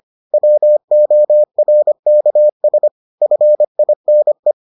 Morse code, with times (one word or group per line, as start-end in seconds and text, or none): 0.3-2.9 WORKS
3.2-4.5 FINE